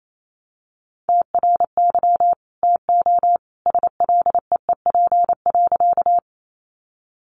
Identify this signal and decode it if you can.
Morse code, heard as a keyed tone, about 28 wpm, 700 Hz. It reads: TRY TO SLEEP.
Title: Sentences from Top 500 Words 28wpm